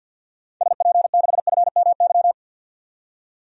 Morse code, heard as a keyed tone, about 50 wpm, 700 Hz. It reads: SP6FMY
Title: DXCC-100 Call Signs - 6 Characters - 50wpm